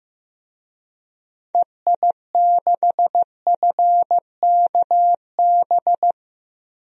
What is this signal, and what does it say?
Morse code, heard as a keyed tone, about 15 wpm, 700 Hz. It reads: EI6FKB
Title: DXCC-100 Call Signs - 6 Characters - Repeated 2x - 15wpm